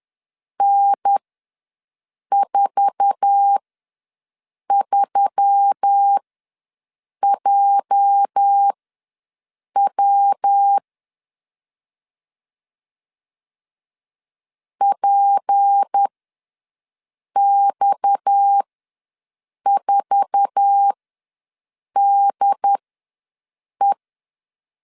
Morse code, heard as a keyed tone, about 10 wpm, spaced as for 6 wpm, 800 Hz. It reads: N43JW PX4DE
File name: LEZIONI / 8 YJ45 / 8 YJ4 ws 10wpm 4020